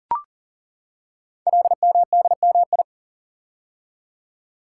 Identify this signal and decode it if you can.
Morse code, heard as a keyed tone, about 40 wpm, 700 Hz. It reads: LMDMI
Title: Code Groups - 5 Letters - 40wpm